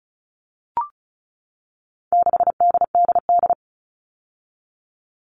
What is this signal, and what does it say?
Morse code, heard as a keyed tone, about 34 wpm, 700 Hz. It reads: 6DDD
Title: All Cut Numbers - 4-Digit 35wpm